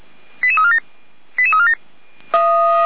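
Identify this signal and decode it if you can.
Five-tone selective call (ZVEI), heard as quick runs of five tones, tones of about 70 ms. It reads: 88347 88347
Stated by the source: ZVEI five-tone selective call 88103 received via SDR, from the audio.f85360.feuerwehr-orlamuende.org fire brigade server.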